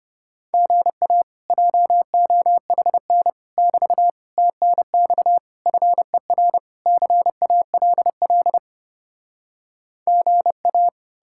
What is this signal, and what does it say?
Morse code, heard as a keyed tone, about 33 wpm, 700 Hz. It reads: GA JOHN <BT> TNX FER CALL GA